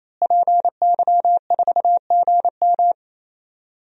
Morse code, heard as a keyed tone, about 28 wpm, 700 Hz. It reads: PY4GM
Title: DXCC-100 Call Signs - 5 Characters - Repeated 3x - 28wpm